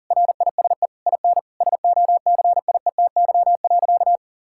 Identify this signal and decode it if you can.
Morse code, heard as a keyed tone, about 40 wpm, 700 Hz. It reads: RISE IN SOCIETY.
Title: Sentences from Top 1000 Words - Review Part 6 - 40wpm